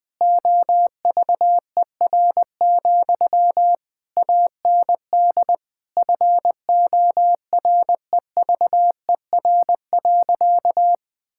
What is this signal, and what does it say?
Morse code, heard as a keyed tone, about 19 wpm, 700 Hz. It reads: OVER, AND FOREVER.
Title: Sentences from Top 1600 Words 20wpm